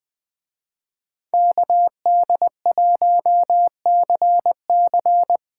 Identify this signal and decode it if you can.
Morse code, heard as a keyed tone, about 20 wpm, 700 Hz. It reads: KD1CC
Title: Speed Racing - USA Call Signs 17wpm